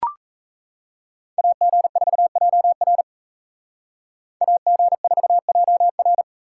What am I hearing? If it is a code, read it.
Morse code, heard as a keyed tone, about 36 wpm, 700 Hz. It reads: AG4JR AG4JR